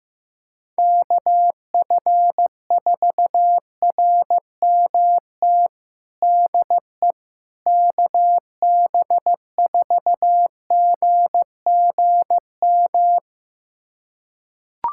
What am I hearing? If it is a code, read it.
Morse code, heard as a keyed tone, about 15 wpm, 700 Hz. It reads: KF4RMT DE KB4GGM
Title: USA Callsign DE Callsign - 6 Characters - 15wpm